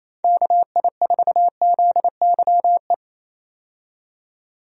Morse code, heard as a keyed tone, about 27 wpm, 700 Hz. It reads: KI4ZYE
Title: USA Call Signs - 6 Characters - 28wpm